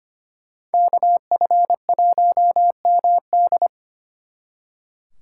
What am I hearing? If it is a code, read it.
Morse code, heard as a keyed tone, about 33 wpm, 700 Hz. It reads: KF1MD